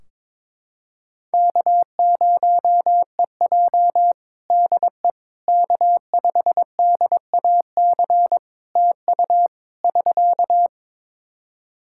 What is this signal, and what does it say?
Morse code, heard as a keyed tone, about 21 wpm, 700 Hz. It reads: K0EJ DE K5DAC TU <SK>
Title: Practice QSO - Exchange 4 - Ending - 22wpm